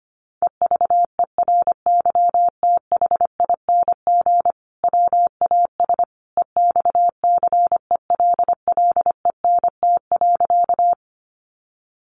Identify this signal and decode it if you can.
Morse code, heard as a keyed tone, about 25 wpm, 700 Hz. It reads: EVERYTHING WAS EXCELLENT.
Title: Sentences from Top 1000 Words 25wpm